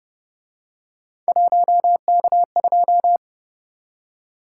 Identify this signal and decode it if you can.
Morse code, heard as a keyed tone, about 29 wpm, 700 Hz. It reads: 1K2